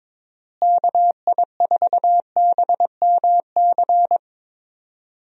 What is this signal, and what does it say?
Morse code, heard as a keyed tone, about 22 wpm, 700 Hz. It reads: KI4BMC
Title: USA Call Signs - 6 Characters - Repeated 3x - 22wpm